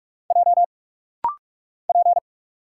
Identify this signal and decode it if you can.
Morse code, heard as a keyed tone, about 46 wpm, 700 Hz. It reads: J P
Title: Single Letter-Number - Rapid-Fire 45wpm